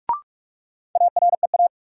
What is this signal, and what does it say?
Morse code, heard as a keyed tone, about 45 wpm, 700 Hz. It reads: AREA
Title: Sentences from Top 1000 Words - Review Part 2 - New Word Drill - 45wpm